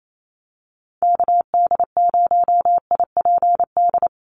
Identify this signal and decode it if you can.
Morse code, heard as a keyed tone, about 28 wpm, 700 Hz. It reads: KD0IPD
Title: USA Call Signs 28wpm